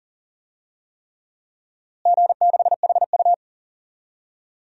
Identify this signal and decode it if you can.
Morse code, heard as a keyed tone, about 40 wpm, 700 Hz. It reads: G6HU
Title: DXCC-100 Call Signs - 4 Characters - Repeated 3x - 40wpm